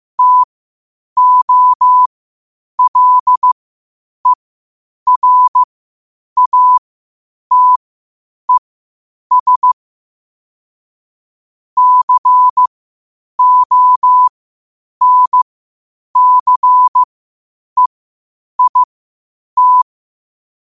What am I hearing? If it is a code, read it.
Morse code, heard as a keyed tone, about 15 wpm, spaced as for 8 wpm, 1000 Hz. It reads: TOLERATES CONCEIT
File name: qotdfiles / 8wpm